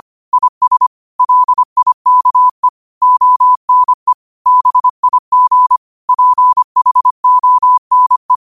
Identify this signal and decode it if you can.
Morse code, heard as a keyed tone, about 25 wpm, 1000 Hz. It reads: IS LIKE ONE BIG PHONE